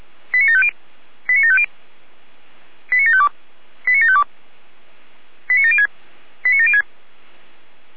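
Five-tone selective call (ZVEI), heard as quick runs of five tones, tones of about 70 ms. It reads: 89750 89750 89752 89752 89706 89706
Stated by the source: ZVEI five-tone call received on SDR.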